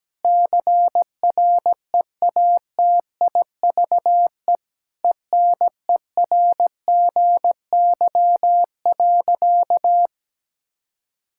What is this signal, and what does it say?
Morse code, heard as a keyed tone, about 17 wpm, 700 Hz. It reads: CREATIVE ENERGY.
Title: Sentences from Top 1000 Words - Review Part 6 - 17wpm